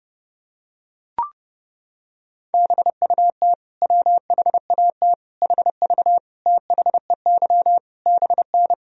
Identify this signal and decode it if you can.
Morse code, heard as a keyed tone, about 32 wpm, 700 Hz. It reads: BUT WHAT HV THEY BN
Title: Sentences from Top 100 Words and Abbreviations 30wpm